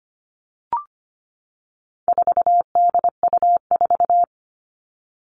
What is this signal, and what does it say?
Morse code, heard as a keyed tone, about 25 wpm, 700 Hz. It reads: 4DU4
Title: All Cut Numbers - 4-Digit 25wpm